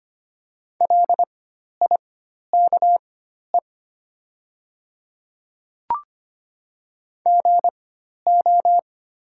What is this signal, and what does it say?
Morse code, heard as a keyed tone, about 25 wpm, spaced as for 12 wpm, 700 Hz. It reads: LIKE GO
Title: 4x Character Spacing - Top 100 Words - 25wpm